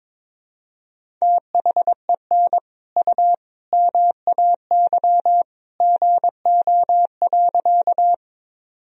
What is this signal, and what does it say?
Morse code, heard as a keyed tone, about 22 wpm, 700 Hz. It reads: THEN U MAY GO.
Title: Sentences from Top 100 Words and Abbreviations 22wpm